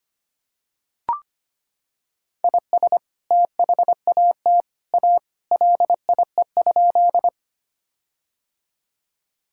Morse code, heard as a keyed tone, about 25 wpm, 700 Hz. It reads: IS THAT A LIE?
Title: Sentences from Top 1600 Words 25wpm